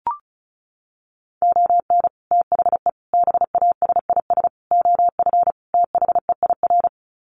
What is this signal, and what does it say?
Morse code, heard as a keyed tone, about 34 wpm, 700 Hz. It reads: ON THE BASIS OF THEIR
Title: Sets of 5 Words Encore 35wpm